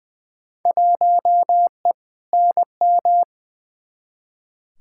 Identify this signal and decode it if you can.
Morse code, heard as a keyed tone, about 20 wpm, 700 Hz. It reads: 1E NM
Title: Field Day Exchanges - 20wpm